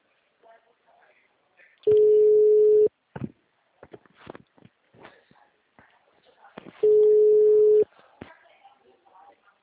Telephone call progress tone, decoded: ringback tone

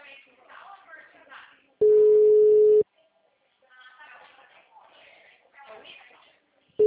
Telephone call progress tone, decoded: ringback tone

